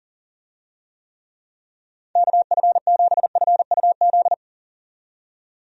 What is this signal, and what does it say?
Morse code, heard as a keyed tone, about 40 wpm, 700 Hz. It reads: KF7FUZ
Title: USA Call Signs - 6 Characters - 40wpm